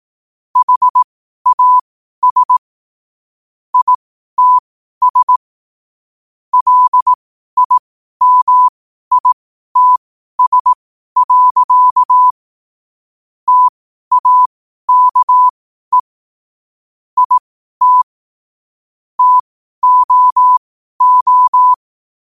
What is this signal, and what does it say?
Morse code, heard as a keyed tone, about 18 wpm, spaced as for 12 wpm, 1000 Hz. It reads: HAS ITS LIMITS. TAKE IT TOO
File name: qotdfiles / 12wpm